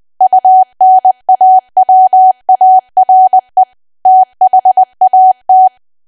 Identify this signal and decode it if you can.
Morse code, heard as a keyed tone, about 20 wpm, 750 Hz. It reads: UNAWARE THAT
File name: Archive / 20 WPM / 20WPM